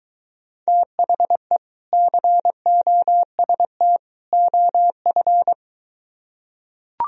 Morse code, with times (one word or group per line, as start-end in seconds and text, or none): 0.7-1.6 THE
1.9-4.0 COST
4.3-5.5 OF